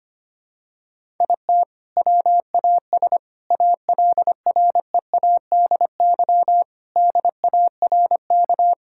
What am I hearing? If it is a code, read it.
Morse code, heard as a keyed tone, about 25 wpm, 700 Hz. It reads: IT WAS ALREADY DARK
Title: Sentences from Top 1000 Words - Review Part 5 - 25wpm